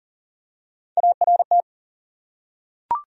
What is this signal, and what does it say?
Morse code, heard as a keyed tone, about 40 wpm, 700 Hz. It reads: ART
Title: USA Common Names - Max 3 Letters - 40wpm